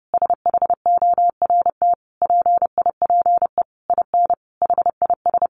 Morse code, heard as a keyed tone, about 30 wpm, 700 Hz. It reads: SHORT PIPE IN HIS
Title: Sentences from Top 1500 Words 30wpm